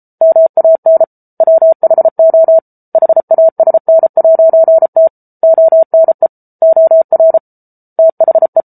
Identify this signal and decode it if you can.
Morse code, heard as a keyed tone, about 34 wpm, 650 Hz. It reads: MAN WHO HASN'T ONE OR THE